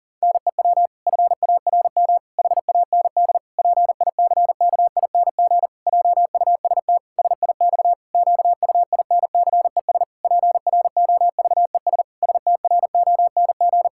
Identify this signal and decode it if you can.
Morse code, heard as a keyed tone, about 37 wpm, 700 Hz. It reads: NEW FARM HAND PICKING JUST SIX QUINCES PROVES STRONG
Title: Pangram Challenge v2 - 40wpm